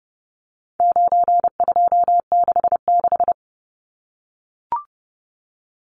Morse code, heard as a keyed tone, about 30 wpm, 700 Hz. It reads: 9266
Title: Numbers - 4-Digit 30wpm